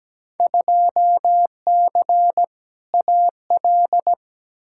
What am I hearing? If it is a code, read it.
Morse code, heard as a keyed tone, about 17 wpm, 700 Hz. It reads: 2C AL
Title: Field Day Exchanges Encore - 17wpm